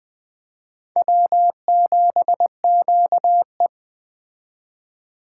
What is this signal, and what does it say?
Morse code, heard as a keyed tone, about 21 wpm, 700 Hz. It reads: W7QE
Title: Speed Racing - USA 1x2 Call Signs 17wpm